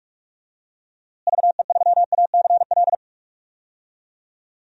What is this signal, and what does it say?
Morse code, heard as a keyed tone, about 46 wpm, 700 Hz. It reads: VE3ACL
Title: DXCC-100 Call Signs - 6 Characters - Repeated 3x - 45wpm